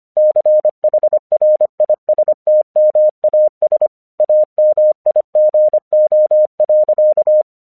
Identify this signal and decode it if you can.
Morse code, heard as a keyed tone, about 25 wpm, 610 Hz. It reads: CHRISTMAS AMIGO.
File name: 1 English 25wpm / AQC31